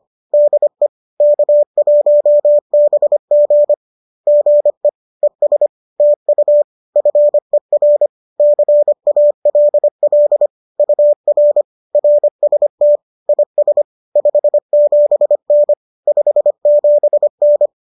Morse code, heard as a keyed tone, about 25 wpm, 590 Hz. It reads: DE K1BG GE ES TU FER CALL UR RST IS 57N 57N